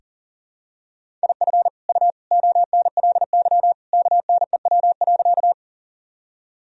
Morse code, heard as a keyed tone, about 40 wpm, 700 Hz. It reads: IF U ONLY KNEW.
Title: Sentences from Top 400 Words and Abbreviations 40wpm